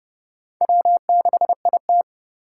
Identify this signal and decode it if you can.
Morse code, heard as a keyed tone, about 29 wpm, 700 Hz. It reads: W6IT